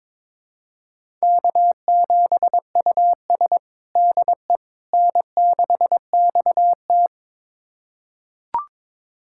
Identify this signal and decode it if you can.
Morse code, heard as a keyed tone, about 22 wpm, 700 Hz. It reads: K7US DE N6XT
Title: USA Callsign DE Callsign - 4 Characters - 22wpm